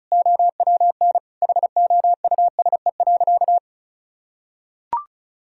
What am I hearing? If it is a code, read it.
Morse code, heard as a keyed tone, about 34 wpm, 690 Hz. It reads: OWN HOUSE.
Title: Sentences from Top 1000 Words - Review Part 1 - 35wpm